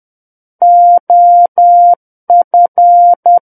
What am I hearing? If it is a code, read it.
Morse code, heard as a keyed tone, about 10 wpm, 700 Hz. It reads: OF
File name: Book One / 01-Chapter 1 - 10 WPM